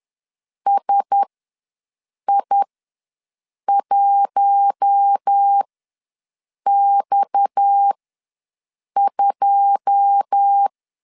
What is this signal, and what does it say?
Morse code, heard as a keyed tone, about 10 wpm, spaced as for 6 wpm, 800 Hz. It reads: SI1X2
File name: LEZIONI / 9 2 ws 10wpm 4130